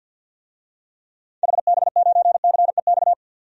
Morse code, heard as a keyed tone, about 52 wpm, 700 Hz. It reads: HB9CEX
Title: DXCC-100 Call Signs - 6 Characters - 50wpm